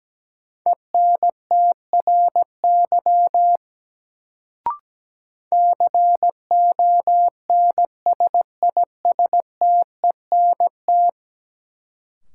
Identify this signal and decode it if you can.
Morse code, heard as a keyed tone, about 17 wpm, 700 Hz. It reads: ENTRY CONSISTENT
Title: Sentences from Top 1500 Words - New Word Drill Alt - 17wpm